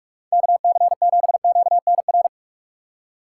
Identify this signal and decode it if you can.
Morse code, heard as a keyed tone, about 46 wpm, 690 Hz. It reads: KC7QNR
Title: USA Call Signs - 6 Characters - 3x Character Spacing - 45wpm